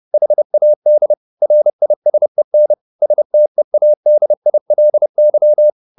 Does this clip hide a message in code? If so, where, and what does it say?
Yes: Morse code from 0.1 to 5.7 s, heard as a keyed tone, about 30 wpm, 600 Hz. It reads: HAD RISEN STEADILY